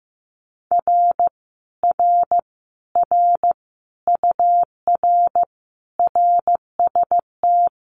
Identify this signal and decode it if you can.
Morse code, heard as a keyed tone, about 16 wpm, 700 Hz. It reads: R R R UR RST
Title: Common QSO Phrases Encore 15wpm